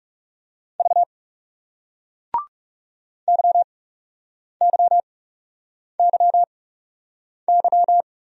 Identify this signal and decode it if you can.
Morse code, heard as a keyed tone, about 38 wpm, spaced as for 8 wpm, 700 Hz. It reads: V YYYY